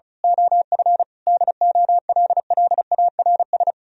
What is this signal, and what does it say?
Morse code, heard as a keyed tone, about 28 wpm, 700 Hz. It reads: OF DOLLARS
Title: Speed-Racing - Sets of 5 Words 30wpm